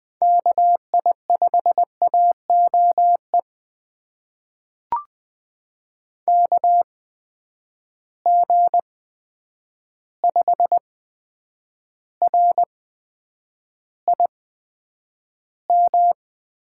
Morse code, heard as a keyed tone, about 20 wpm, spaced as for 9 wpm, 700 Hz. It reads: KI5AOE KG5RIM
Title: USA Call Signs - 6 Characters - 8x Character Spacing - 20wpm